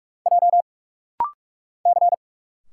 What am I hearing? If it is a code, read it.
Morse code, heard as a keyed tone, about 44 wpm, 700 Hz. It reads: J C
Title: Single Letters - Rapid-Fire 45wpm